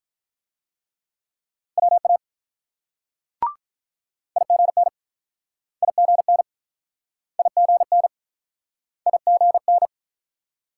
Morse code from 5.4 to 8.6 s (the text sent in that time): IGN IGN